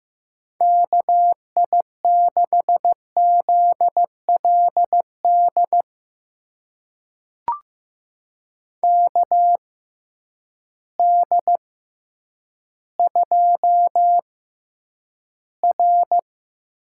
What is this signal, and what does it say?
Morse code, heard as a keyed tone, about 15 wpm, spaced as for 9 wpm, 700 Hz. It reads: KI6ZLD KD2R
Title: USA Call Signs - 6x Character Spacing - 15wpm